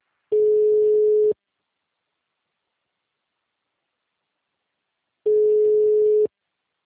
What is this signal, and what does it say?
Ringback tone (Europe, 425 Hz).